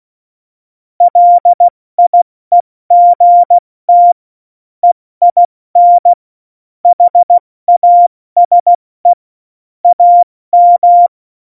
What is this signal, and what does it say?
Morse code, heard as a keyed tone, about 16 wpm, 700 Hz. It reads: LIEGT EIN HASE AM
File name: Morsekurs / Witze / Witz 08. Dezember 2020, 70 F80